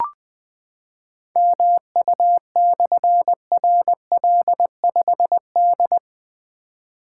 Morse code, heard as a keyed tone, about 20 wpm, 700 Hz. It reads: MU/RL5D
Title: Difficult Call Signs 20wpm